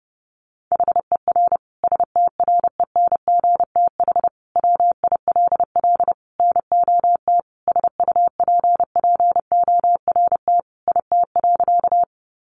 Morse code, heard as a keyed tone, about 30 wpm, 700 Hz. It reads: HER STRENGTH WILL NOT SUPPORT IT.
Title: Sentences from Top 1300 Words 30wpm